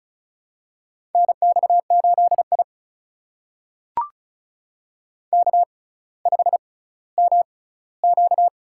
Morse code, heard as a keyed tone, about 36 wpm, spaced as for 18 wpm, 690 Hz. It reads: NX8I K5MQ
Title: USA Call Signs - 4 Characters - 6x Character Spacing - 35wpm